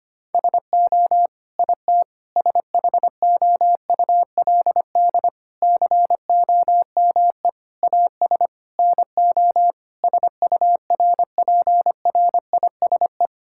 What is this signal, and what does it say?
Morse code, heard as a keyed tone, about 25 wpm, 700 Hz. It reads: SO IT SHOULD COME AS NO SURPRISE
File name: Sets of 9 Words - 25wpm